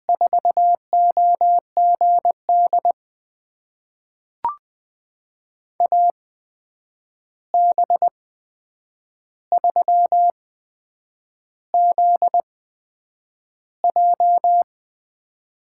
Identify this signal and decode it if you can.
Morse code, heard as a keyed tone, about 20 wpm, spaced as for 8 wpm, 700 Hz. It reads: 4OGD AB3ZJ
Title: USA Call Signs - 8x Character Spacing - 20wpm